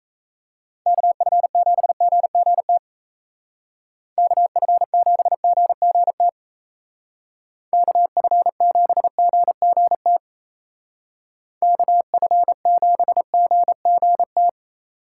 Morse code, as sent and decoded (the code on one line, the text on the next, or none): -.- ..-. --... --. --. - / -.- ..-. --... --. --. - / -.- ..-. --... --. --. - / -.- ..-. --... --. --. -
KF7GGT KF7GGT KF7GGT KF7GGT